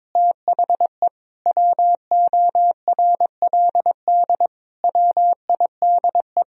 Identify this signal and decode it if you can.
Morse code, heard as a keyed tone, about 22 wpm, 700 Hz. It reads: THE WORLD WIDE
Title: Sentences from Top 1000 Words - Review Part 4 - 22wpm